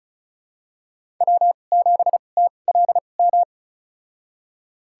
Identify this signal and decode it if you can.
Morse code, heard as a keyed tone, about 36 wpm, spaced as for 25 wpm, 700 Hz. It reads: W7TLM
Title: USA Call Signs - 5 Characters - 2x Character Spacing - 35wpm